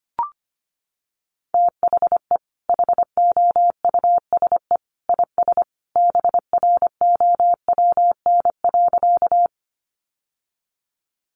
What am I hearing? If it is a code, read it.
Morse code, heard as a keyed tone, about 25 wpm, 700 Hz. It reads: THE HOUSE IS BROWN.
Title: Sentences from Top 1200 Words 25wpm